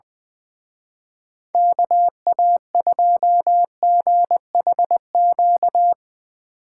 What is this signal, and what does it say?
Morse code, heard as a keyed tone, about 22 wpm, 700 Hz. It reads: KA2GHQ
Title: Speed Racing - USA 2x3 Call Signs 20wpm